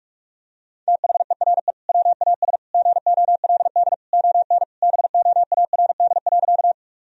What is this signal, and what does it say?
Morse code, heard as a keyed tone, about 46 wpm, 700 Hz. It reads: THERE WAS GOLD ON BOARD.
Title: Sentences from Top 1000 Words - Review Part 4 - 45wpm